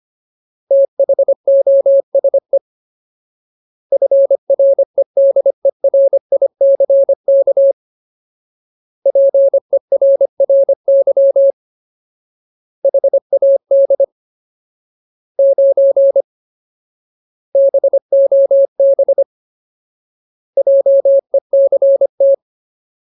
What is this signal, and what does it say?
Morse code, heard as a keyed tone, about 25 wpm, 550 Hz. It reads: THOSE FREDERICK PERRY HAD 9 BOB JECT
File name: Ultimate Word List / 25WPM Ultimate